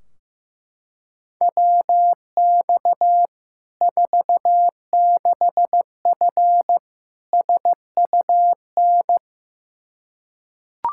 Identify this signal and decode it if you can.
Morse code, heard as a keyed tone, about 15 wpm, 700 Hz. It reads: WX 46F SUN
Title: Practice QSO - Exchange 2 - Info - 15wpm